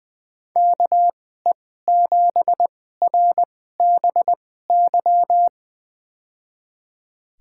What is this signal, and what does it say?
Morse code, heard as a keyed tone, about 20 wpm, spaced as for 14 wpm, 700 Hz. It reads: KE7RBY
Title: USA Call Signs - 6 Characters - 2x Character Spacing - 20wpm